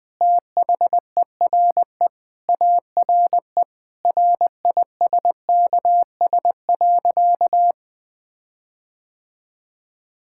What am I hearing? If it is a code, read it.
Morse code, heard as a keyed tone, about 21 wpm, 700 Hz. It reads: THERE ARE RISKS.
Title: Sentences with 5-Word Phrases - Spoken Beginning - 20wpm